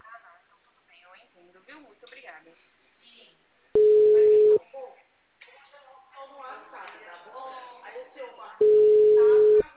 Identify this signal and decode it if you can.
Ringback tone (Europe, 425 Hz), with speech.
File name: out- -2007- - - .23383